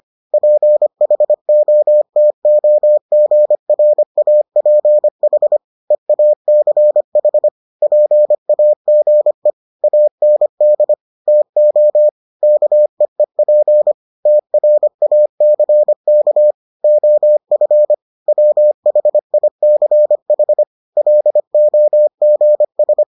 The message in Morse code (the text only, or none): PHOTOGRAPH EACH PAGE AND TO KEEP TRACK OF WHICH LOGS